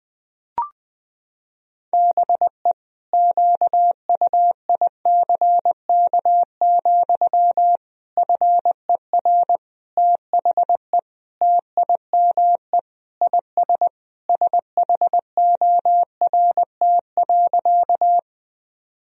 Morse code, heard as a keyed tone, about 20 wpm, 700 Hz. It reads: BE QUICK, FER THE TIME IS SHORT.
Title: Sentences from Top 500 Words and Abbreviations 20wpm